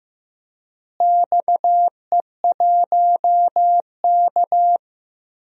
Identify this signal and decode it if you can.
Morse code, heard as a keyed tone, about 15 wpm, 700 Hz. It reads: XE1K
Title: DXCC-100 Call Signs - 4 Characters - Repeated 2x - 15wpm